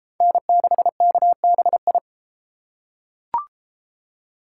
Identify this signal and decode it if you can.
Morse code, heard as a keyed tone, about 30 wpm, 700 Hz. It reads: N6KBI